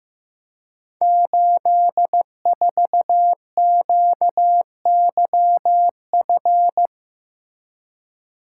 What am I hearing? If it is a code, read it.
Morse code, heard as a keyed tone, about 15 wpm, 700 Hz. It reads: 84QYF
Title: Code Groups - 5 Letter:Numbers - 15wpm